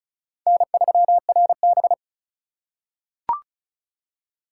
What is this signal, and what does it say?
Morse code, heard as a keyed tone, about 34 wpm, 700 Hz. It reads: N3RB
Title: USA Call Signs - 4 Characters - 4x Character Spacing - 35wpm